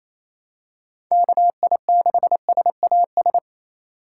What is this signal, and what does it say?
Morse code, heard as a keyed tone, about 27 wpm, 700 Hz. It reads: KI6SAS
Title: USA Call Signs - 6 Characters - 4x Character Spacing - 28wpm